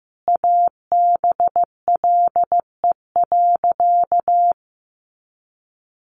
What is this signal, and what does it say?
Morse code, heard as a keyed tone, about 15 wpm, 700 Hz. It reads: ABLE.
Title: Sentences from Top 1300 Words 15wpm